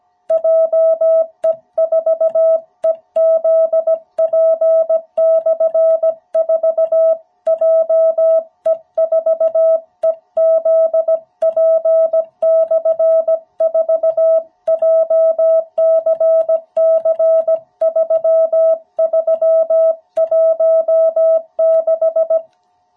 Morse code, heard as a keyed tone, about 17 wpm, 660 Hz. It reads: JE4EZP/4 JE4EZP/4 JCC3316
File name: jh8cbh / 2021cw